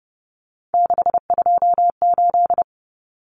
Morse code, heard as a keyed tone, about 30 wpm, 700 Hz. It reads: 628
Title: Common Cut Numbers - 3-Digit 30wpm